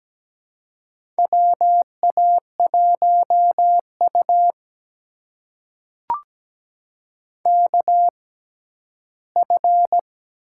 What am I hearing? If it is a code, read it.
Morse code, heard as a keyed tone, about 17 wpm, spaced as for 11 wpm, 700 Hz. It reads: WA1U KF
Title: USA Call Signs - 6x Character Spacing - 17wpm